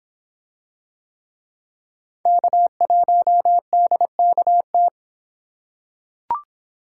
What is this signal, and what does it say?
Morse code, heard as a keyed tone, about 25 wpm, 700 Hz. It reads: K1DKT